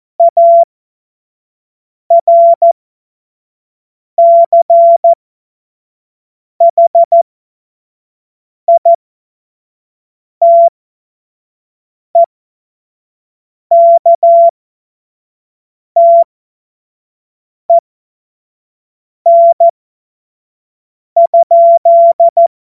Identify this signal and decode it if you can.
Morse code, heard as a keyed tone, about 14 wpm, spaced as for 5 wpm, 680 Hz. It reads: ARCHITEKTEN?
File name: Morsekurs / Witze / Witz 27. Januar 2026 25 F70